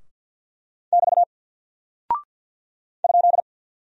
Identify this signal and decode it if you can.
Morse code, heard as a keyed tone, about 48 wpm, 700 Hz. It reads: <BT> ?
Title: Single Character (Letter:Number:Punctuation:Prosign) Alt 50wpm